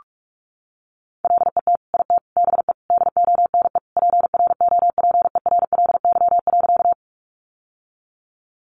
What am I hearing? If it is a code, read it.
Morse code, heard as a keyed tone, about 46 wpm, 700 Hz. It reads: LET IT BE DONE PROPERLY.
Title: Sentences from Top 1100 Words 45wpm